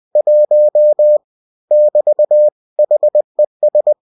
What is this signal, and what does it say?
Morse code, heard as a keyed tone, about 20 wpm, 600 Hz. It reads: 1 <BT> HES